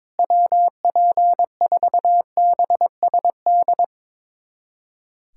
Morse code, heard as a keyed tone, about 21 wpm, 700 Hz. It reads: WP4BSD
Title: USA Callsign DE Callsign 22wpm